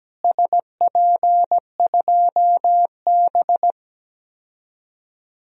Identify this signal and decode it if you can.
Morse code, heard as a keyed tone, about 17 wpm, 700 Hz. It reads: SP2B
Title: DXCC-100 Call Signs - 4 Characters - Repeated 2x - 17wpm